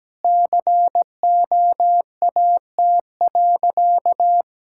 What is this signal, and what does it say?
Morse code, heard as a keyed tone, about 17 wpm, 700 Hz. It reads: COAT.
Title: Sentences from Top 1600 Words 17wpm